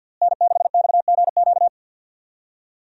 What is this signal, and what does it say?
Morse code, heard as a keyed tone, about 51 wpm, 700 Hz. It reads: N6XGQ
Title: USA Call Signs - 5 Characters - Repeated 3x - 50wpm